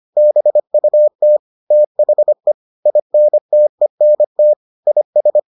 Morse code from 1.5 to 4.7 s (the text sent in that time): THE INTENT